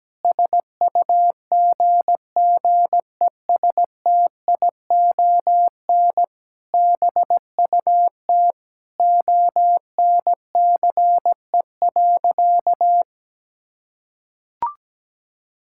Morse code, heard as a keyed tone, about 17 wpm, 700 Hz. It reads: SUGGESTION BUT ONCE.
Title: Sentences from Top 1600 Words 17wpm